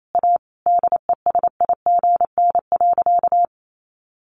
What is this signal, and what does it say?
Morse code, heard as a keyed tone, about 28 wpm, 700 Hz. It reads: A DESIGN.
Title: Sentences from Top 800 Words 28wpm